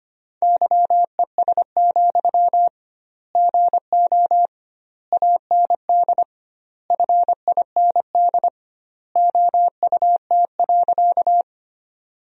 Morse code, heard as a keyed tone, about 25 wpm, 700 Hz. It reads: YES, GO AND FIND OUT.
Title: 2x Word-Spacing - Sentences from Top 500 Words - 25wpm